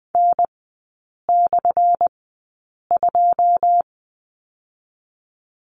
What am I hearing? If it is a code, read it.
Morse code, heard as a keyed tone, about 20 wpm, spaced as for 8 wpm, 700 Hz. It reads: N/2
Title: 191 Lesson 31e - Letter v - Three Characters 20wpm